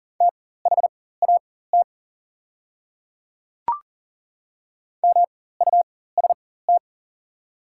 Morse code, heard as a keyed tone, about 37 wpm, spaced as for 18 wpm, 700 Hz. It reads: THAT MUST
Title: 4x Character Spacing - Top 200 Words - 40wpm